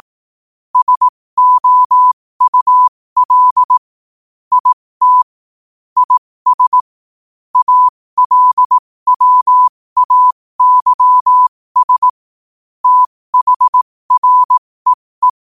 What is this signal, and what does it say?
Morse code, heard as a keyed tone, about 18 wpm, spaced as for 15 wpm, 1000 Hz. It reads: SOUL IT IS ALWAYS THREE
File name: qotdfiles / 15wpm